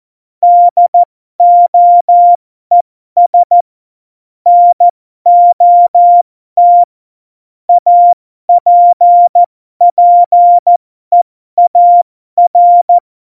Morse code, heard as a keyed tone, about 14 wpm, spaced as for 12 wpm, 700 Hz. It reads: DOES NOT APPEAR
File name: Morsekurs / Witze